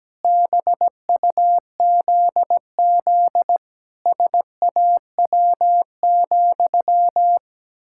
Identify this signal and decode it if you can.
Morse code, heard as a keyed tone, about 17 wpm, 700 Hz. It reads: BUZZ SAW,
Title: Pangram Challenge v2 - 17wpm